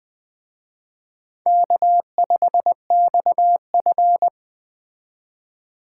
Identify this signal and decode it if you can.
Morse code, heard as a keyed tone, about 20 wpm, 700 Hz. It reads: K5XF
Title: Speed Racing - USA 1x2 Call Signs 17wpm